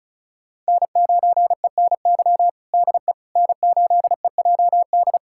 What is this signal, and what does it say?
Morse code, heard as a keyed tone, about 36 wpm, 700 Hz. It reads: N9ENY DE N8EJD
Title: USA Callsign DE Callsign - 5 Characters - 35wpm